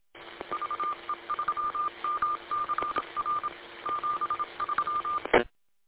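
Morse code, heard as a keyed tone, about 26 wpm, 1200 Hz, with noise. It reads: VE3MCR 73
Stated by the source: Morse code recorded from W8LAP on 146.620.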